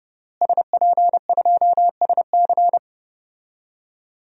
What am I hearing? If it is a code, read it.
Morse code, heard as a keyed tone, about 30 wpm, 700 Hz. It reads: SP2SC
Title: DXCC-100 Call Signs - 5 Characters - Repeated 3x - 30wpm